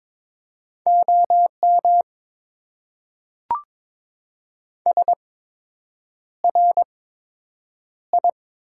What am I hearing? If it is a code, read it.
Morse code, heard as a keyed tone, about 22 wpm, spaced as for 8 wpm, 690 Hz. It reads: OM SRI
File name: Top 90 QSO Elements - 8x Character Spacing - 22wpm